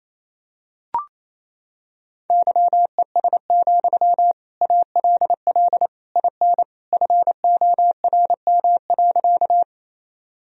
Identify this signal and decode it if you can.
Morse code, heard as a keyed tone, about 28 wpm, 700 Hz. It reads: YES, ALL IN FORM.
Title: Sentences from Top 1000 Words - Review Part 1 - 28wpm